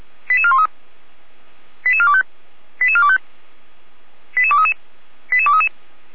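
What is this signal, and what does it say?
Five-tone selective call (ZVEI), heard as quick runs of five tones, tones of about 70 ms. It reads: 88413 88425 88425 88233 88233
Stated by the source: ZVEI five-tone selective call received by SDR.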